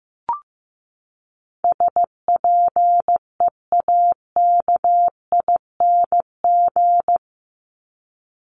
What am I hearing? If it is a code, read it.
Morse code, heard as a keyed tone, about 15 wpm, 700 Hz. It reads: SPEAKING
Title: ING Suffix - Top 100 Words with Base Words 15wpm